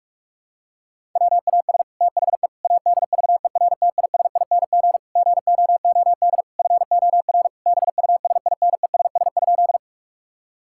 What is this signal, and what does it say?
Morse code, heard as a keyed tone, about 44 wpm, 700 Hz. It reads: WAS THE ADVERTISING GOOD FOR BUSINESS?
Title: Sentences from Top 1000 Words - Review Part 6 - 45wpm